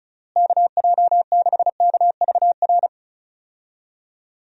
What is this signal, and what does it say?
Morse code, heard as a keyed tone, about 36 wpm, 700 Hz. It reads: KJ6KVR
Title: USA Call Signs - 6 Characters - Repeated 2x - 35wpm